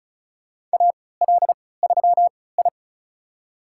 Morse code, heard as a keyed tone, about 34 wpm, spaced as for 20 wpm, 700 Hz. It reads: AL3I
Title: USA Call Signs - 4 Characters - 3x Character Spacing - 35wpm